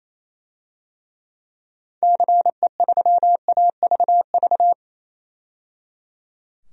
Morse code, heard as a keyed tone, about 28 wpm, 700 Hz. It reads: CE3AVV
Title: DXCC-100 Call Signs - 6 Characters - 28wpm